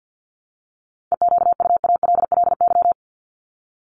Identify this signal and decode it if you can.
Morse code, heard as a keyed tone, about 49 wpm, 700 Hz. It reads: EQUALLY
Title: LY Suffix - Top 100 Words with Base Words 50wpm